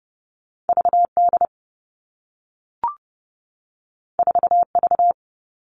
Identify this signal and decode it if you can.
Morse code, heard as a keyed tone, about 30 wpm, 700 Hz. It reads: VD 4V
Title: All Cut Numbers - 2-Digit 30wpm